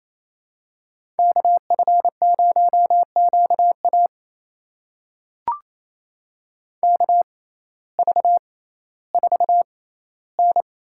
Morse code, heard as a keyed tone, about 28 wpm, spaced as for 15 wpm, 700 Hz. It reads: KF0QA KV4N
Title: USA Call Signs - 5 Characters - 6x Character Spacing - 28wpm